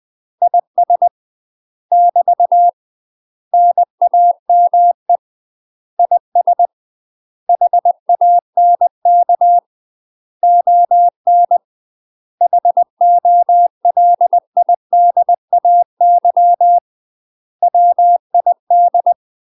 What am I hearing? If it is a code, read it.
Morse code, heard as a keyed tone, about 20 wpm, 700 Hz. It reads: IS <BT> NAME IS HANK ON HOLIDAY WID